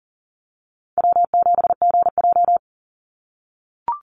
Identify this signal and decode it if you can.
Morse code, heard as a keyed tone, about 40 wpm, 700 Hz. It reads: W7GJ
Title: NA SOTA Chaser Callsigns 40wpm